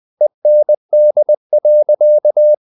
Morse code, heard as a keyed tone, about 20 wpm, 600 Hz. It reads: END.
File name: the old man and the sea part 2 20wpm